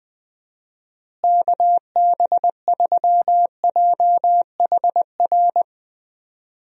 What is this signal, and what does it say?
Morse code, heard as a keyed tone, about 20 wpm, 700 Hz. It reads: KB3JHR
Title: USA Call Signs - 6 Characters - Repeated 2x - 20wpm